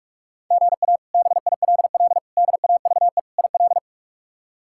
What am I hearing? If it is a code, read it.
Morse code, heard as a keyed tone, about 46 wpm, 700 Hz. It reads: GA BILL DAVE IL
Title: K1USN SST QSO Practice - 45wpm